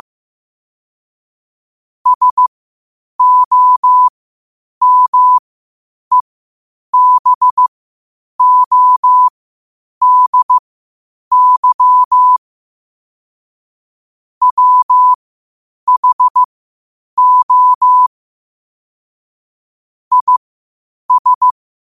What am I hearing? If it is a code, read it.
Morse code, heard as a keyed tone, about 15 wpm, spaced as for 8 wpm, 1000 Hz. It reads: SOMEBODY WHO IS
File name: qotdfiles / 8wpm